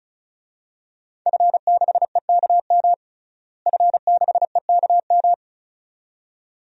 Morse code, heard as a keyed tone, about 36 wpm, 700 Hz. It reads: F6EKM F6EKM
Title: DXCC-100 Call Signs - 5 Characters - Repeated 2x - 35wpm